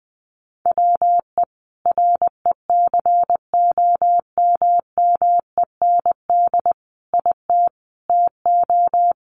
Morse code, heard as a keyed tone, about 20 wpm, 700 Hz. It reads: WE RECOMMEND IT TO
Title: Sentences from Top 1200 Words 20wpm